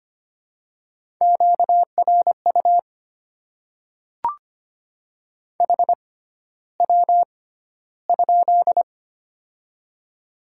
Morse code, heard as a keyed tone, about 25 wpm, spaced as for 13 wpm, 690 Hz. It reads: QRU HW?
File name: Top 90 QSO Elements - 6x Character Spacing - 25wpm